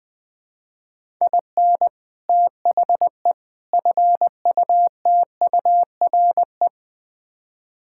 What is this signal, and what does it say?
Morse code, heard as a keyed tone, about 20 wpm, 700 Hz. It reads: IN THE FUTURE
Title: Build A Phrase - 3-Word Phrases - 20wpm